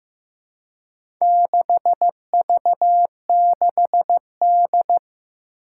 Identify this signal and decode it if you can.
Morse code, heard as a keyed tone, about 15 wpm, 700 Hz. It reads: 6V6D
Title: Four Challenging Letter-Number Combinations 15wpm